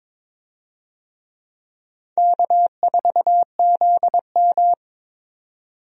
Morse code, heard as a keyed tone, about 22 wpm, 700 Hz. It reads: K4ZM
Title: USA Call Signs - 4 Characters - 22wpm